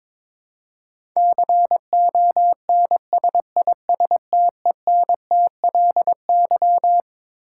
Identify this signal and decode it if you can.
Morse code, heard as a keyed tone, about 22 wpm, 700 Hz. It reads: CONSISTENTLY
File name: Twelve Letter Words - 22wpm